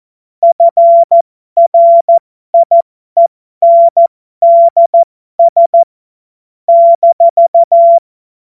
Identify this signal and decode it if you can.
Morse code, heard as a keyed tone, about 14 wpm, spaced as for 12 wpm, 680 Hz. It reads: FRIENDS -
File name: Morsekurs / Witze